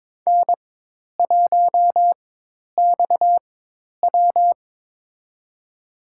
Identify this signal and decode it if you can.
Morse code, heard as a keyed tone, about 22 wpm, spaced as for 10 wpm, 700 Hz. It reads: N1XW
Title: USA Call Signs - 4 Characters - 4x Character Spacing - 22wpm